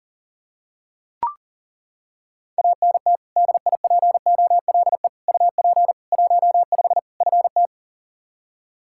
Morse code, heard as a keyed tone, about 40 wpm, 700 Hz. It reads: ANT DIPOLE UP 15 FT
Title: Practice QSO - Exchange 2 - Info - 40wpm